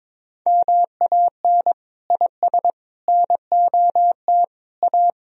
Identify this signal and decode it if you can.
Morse code, heard as a keyed tone, about 22 wpm, 700 Hz. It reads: MAN IS NOT A
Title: Sentences from Top 1000 Words - Review Part 2 - 22wpm